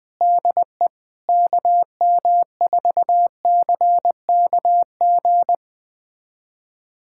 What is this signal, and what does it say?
Morse code, heard as a keyed tone, about 20 wpm, 700 Hz. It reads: DE KM4CKG
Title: USA Callsign DE Callsign 20wpm